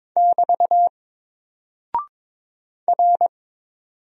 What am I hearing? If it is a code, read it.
Morse code, heard as a keyed tone, about 22 wpm, 700 Hz. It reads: <BT> R
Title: Single Character (Letter:Number:Punctuation:Prosign) Alt 22wpm